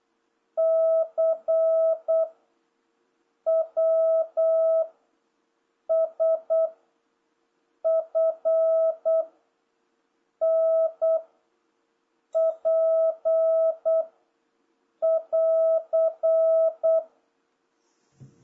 Morse code, heard as a keyed tone, about 8 wpm, 650 Hz. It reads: C W S F N P <AR>